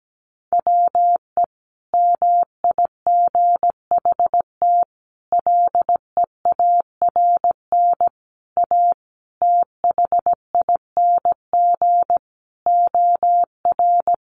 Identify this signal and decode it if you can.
Morse code, heard as a keyed tone, about 17 wpm, 700 Hz. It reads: WE MIGHT LEARN A THING OR
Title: Sentences from Top 300 Words 17wpm